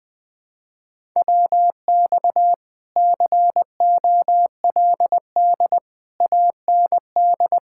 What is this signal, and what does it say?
Morse code, heard as a keyed tone, about 20 wpm, 700 Hz. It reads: WX COLD AND
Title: Speed-Racing Common QSO Phrases Encore - 20wpm